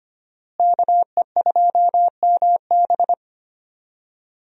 Morse code, heard as a keyed tone, about 25 wpm, 700 Hz. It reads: KE2MB
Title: USA Call Signs - 5 Characters - Repeated 2x - 25wpm